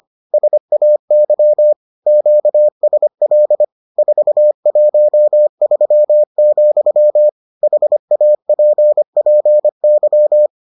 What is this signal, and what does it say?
Morse code, heard as a keyed tone, about 25 wpm, 600 Hz. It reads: SAY QSL 413, HAPPY